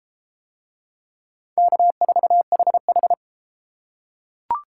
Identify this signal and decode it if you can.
Morse code, heard as a keyed tone, about 29 wpm, 700 Hz. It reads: K4HH